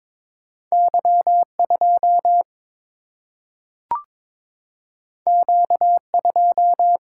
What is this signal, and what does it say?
Morse code, heard as a keyed tone, about 22 wpm, 700 Hz. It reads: Y 2 Q 2
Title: Single-Letter Single-Number 22wpm